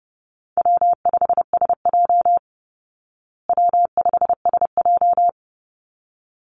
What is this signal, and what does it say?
Morse code, heard as a keyed tone, about 30 wpm, 700 Hz. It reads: W5SJ W5SJ
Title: Call Signs Repeated 3x - 30wpm